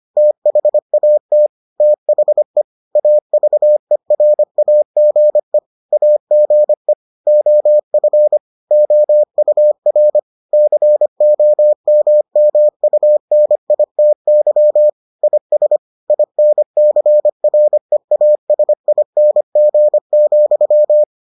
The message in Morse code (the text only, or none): THAT THE AVERAGE AGE OF OUR COMMUNITY IS INCREASING,